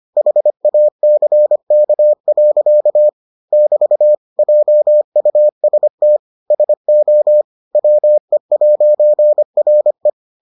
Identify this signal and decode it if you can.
Morse code, heard as a keyed tone, about 25 wpm, 600 Hz. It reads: HACK. <BT> JUST SO WE'RE